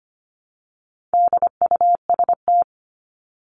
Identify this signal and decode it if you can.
Morse code, heard as a keyed tone, about 25 wpm, 700 Hz. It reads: DUST
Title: Top 1751-2000 Words 25wpm